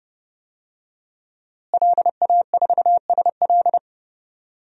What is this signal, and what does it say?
Morse code, heard as a keyed tone, about 30 wpm, 700 Hz. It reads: LA4SL
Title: DXCC-100 Call Signs - 5 Characters - Repeated 3x - 30wpm